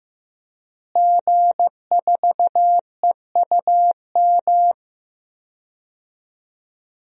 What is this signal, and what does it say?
Morse code, heard as a keyed tone, about 15 wpm, 700 Hz. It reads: G4EUM
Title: DXCC-100 Call Signs - 5 Characters - Repeated 3x - 15wpm